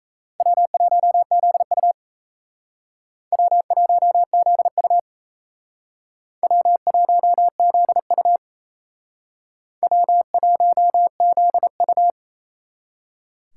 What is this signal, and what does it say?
Morse code, heard as a keyed tone, about 35 wpm, 700 Hz. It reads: W1ZU W1ZU W1ZU W1ZU